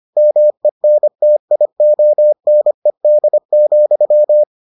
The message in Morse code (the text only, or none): MENTIONED,